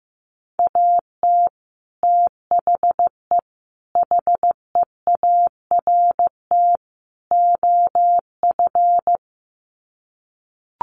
Morse code, heard as a keyed tone, about 15 wpm, 700 Hz. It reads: AT THE HEART OF
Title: Sets of 4 Words 15wpm